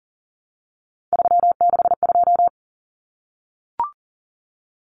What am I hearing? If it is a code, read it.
Morse code, heard as a keyed tone, about 40 wpm, 700 Hz. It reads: 362
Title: Common Cut Numbers - 3-Digit 40wpm